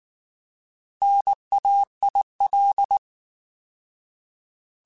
Morse code, heard as a keyed tone, about 19 wpm, 800 Hz. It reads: NAIL
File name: Alphabetical / Alphabetical mp3s / Ns @18